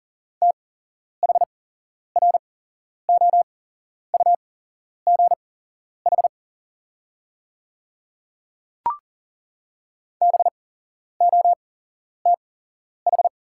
Morse code, heard as a keyed tone, about 40 wpm, spaced as for 11 wpm, 690 Hz. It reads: THROUGH BOTH
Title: 8x Character Spacing - Top 500 Words - 40wpm